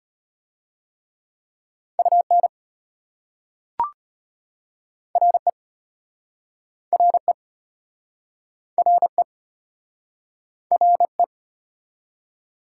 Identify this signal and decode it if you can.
Morse code, heard as a keyed tone, about 34 wpm, 700 Hz. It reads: UN RE RE RE RE